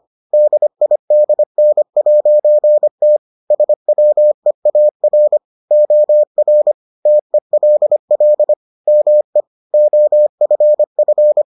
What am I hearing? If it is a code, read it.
Morse code, heard as a keyed tone, about 25 wpm, 600 Hz. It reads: DIDN'T SWEAR OR TELL ME OFF